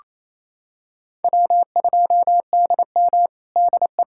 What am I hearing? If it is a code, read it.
Morse code, heard as a keyed tone, about 29 wpm, 700 Hz. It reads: W2DM DE
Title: USA Callsign DE Callsign - 4 Characters - 28wpm